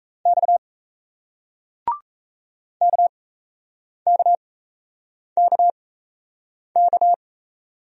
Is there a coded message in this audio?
Yes: Morse code from 0.3 to 7.1 s, heard as a keyed tone, about 38 wpm, spaced as for 8 wpm, 700 Hz. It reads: X KKKK